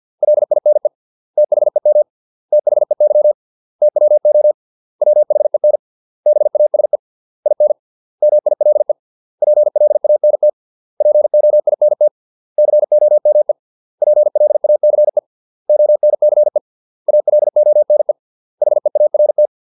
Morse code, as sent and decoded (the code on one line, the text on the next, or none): .-.. .. -. / - .... -- / - .... -.-- / - .-- --- / .-- .... -. / -... .- ... / .. -. / -- .. .-.. / .--. .-.. .- -. - / .--. --- .. -. - / -.-. --- -- . / .--. .-.. .- -.-. / --- -. -.-. / .- .-.. --- -. / .... .- .-. -
LINE THEM THEY TWO WHEN BASE IN MILE PLANT POINT COME PLACE ONCE ALONE HEART